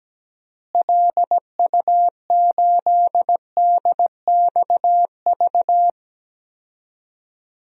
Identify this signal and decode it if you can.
Morse code, heard as a keyed tone, about 17 wpm, 700 Hz. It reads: LU8DXV
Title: DXCC-100 Call Signs - 6 Characters - Repeated 2x - 17wpm